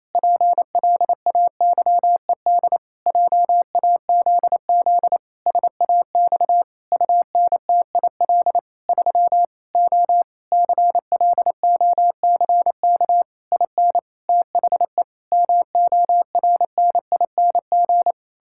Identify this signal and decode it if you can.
Morse code, heard as a keyed tone, about 28 wpm, 700 Hz. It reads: PLAYED JAZZ SAX UNTIL 3 O CLOCK IN THE MORNING
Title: Pangram Challenge v2 - 28wpm